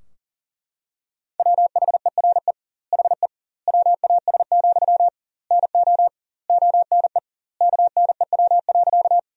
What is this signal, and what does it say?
Morse code, heard as a keyed tone, about 37 wpm, 700 Hz. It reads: WHERE HE WAS, NO ONE KNEW.
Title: 2x Word-Spacing - Sentences from Top 400 Words - 40wpm